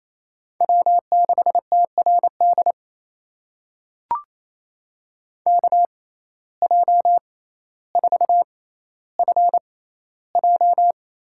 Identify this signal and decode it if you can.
Morse code, heard as a keyed tone, about 28 wpm, spaced as for 14 wpm, 700 Hz. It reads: W6TRD KJ4FJ
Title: USA Call Signs - 6x Character Spacing - 28wpm